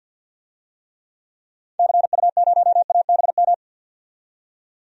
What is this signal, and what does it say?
Morse code, heard as a keyed tone, about 49 wpm, 700 Hz. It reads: CU0ABM
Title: DXCC-100 Call Signs - 6 Characters - 50wpm